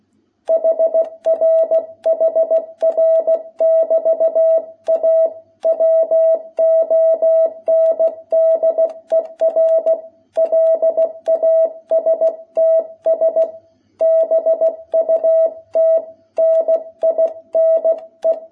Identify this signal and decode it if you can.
Morse code, heard as a keyed tone, about 16 wpm, 660 Hz. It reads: HR HR<BT>A WONDER LASTS BUT NINE
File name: jh8cbh / 2021cw